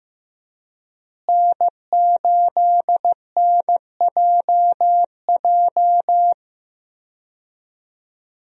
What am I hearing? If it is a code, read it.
Morse code, heard as a keyed tone, about 15 wpm, 700 Hz. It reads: N8NJJ
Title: USA Call Signs - 5 Characters - Repeated 2x - 15wpm